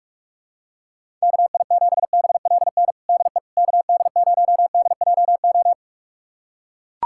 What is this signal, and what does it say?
Morse code, heard as a keyed tone, about 41 wpm, 690 Hz. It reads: KI7BLN DE KD0DJO
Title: USA Callsign DE Callsign 45wpm